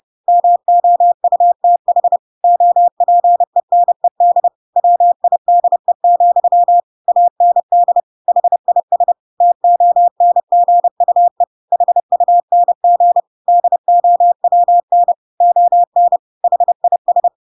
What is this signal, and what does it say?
Morse code, heard as a keyed tone, about 30 wpm, 700 Hz. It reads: MOUTH OPENED WIDE, AND HIS TONGUE HUNG DOWN ON HIS
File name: Pinocchio / Pinocchio ch1 30 30 ik1hge